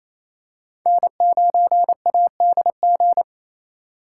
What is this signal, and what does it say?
Morse code, heard as a keyed tone, about 27 wpm, 700 Hz. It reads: N9ADG
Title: K1USN SST QSO Practice - 28wpm